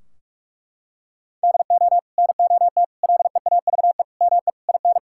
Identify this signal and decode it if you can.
Morse code, heard as a keyed tone, about 41 wpm, 700 Hz. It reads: DO NOT LEAVE ME IN
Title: Sentences from Top 1600 Words 45wpm